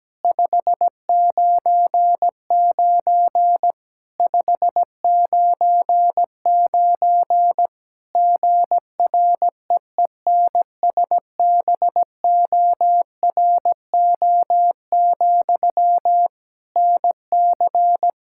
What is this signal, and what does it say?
Morse code, heard as a keyed tone, about 17 wpm, 700 Hz. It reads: 599 599 GREENSBORO, NC
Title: K1USN SST QSO Practice 17wpm